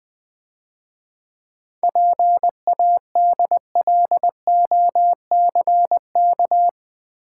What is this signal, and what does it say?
Morse code, heard as a keyed tone, about 20 wpm, 700 Hz. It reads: PADLOCK
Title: Easy Words - Speed-Racing with Hints - 20wpm